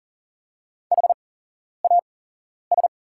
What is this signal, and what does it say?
Morse code, heard as a keyed tone, about 40 wpm, spaced as for 11 wpm, 700 Hz. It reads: HAS
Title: 8x Character Spacing - Top 200 Words - 40wpm